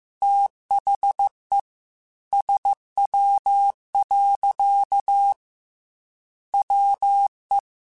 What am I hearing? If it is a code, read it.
Morse code, heard as a keyed tone, about 15 wpm, 800 Hz. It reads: THE SW. WE
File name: Weather Sets / Weather Sets-mp3s / Weather 5 5 @13